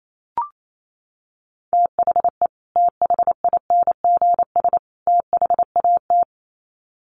Morse code, heard as a keyed tone, about 28 wpm, 700 Hz. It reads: THE THINGS THAT
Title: Sets of 3 Words 28wpm